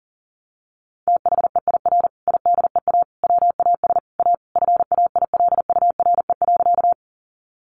Morse code, heard as a keyed tone, about 40 wpm, 700 Hz. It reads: THEIR IDEA WAS A FAILURE.
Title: Sentences from Top 900 Words 40wpm